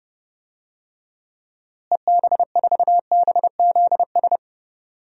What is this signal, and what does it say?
Morse code, heard as a keyed tone, about 30 wpm, 700 Hz. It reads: EB4BZS
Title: DXCC-100 Call Signs - 6 Characters - Repeated 3x - 30wpm